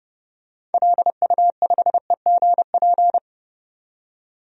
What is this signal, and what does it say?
Morse code, heard as a keyed tone, about 30 wpm, 700 Hz. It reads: LU5EGP
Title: DXCC-100 Call Signs - 6 Characters - 30wpm